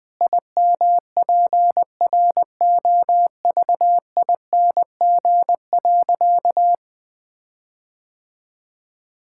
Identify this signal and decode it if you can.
Morse code, heard as a keyed tone, about 20 wpm, 700 Hz. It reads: IMPROVING.
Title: Sentences with 8-Word Phrases - 20wpm